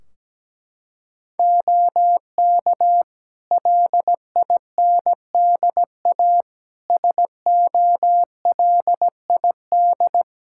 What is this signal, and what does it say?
Morse code, heard as a keyed tone, about 17 wpm, 700 Hz. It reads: OK LINDA SOLID
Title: Practice QSO - Exchange 3 - More Info - 17wpm